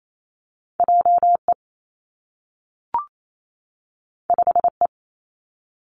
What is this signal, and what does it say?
Morse code, heard as a keyed tone, about 28 wpm, 700 Hz. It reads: J E 5 E
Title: Two Challenging Letter-Number Combinations 28wpm